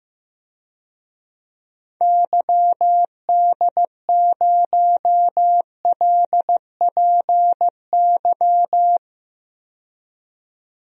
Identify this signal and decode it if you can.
Morse code, heard as a keyed tone, about 15 wpm, 700 Hz. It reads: YD0LPY
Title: DXCC-100 Call Signs - 6 Characters - 15wpm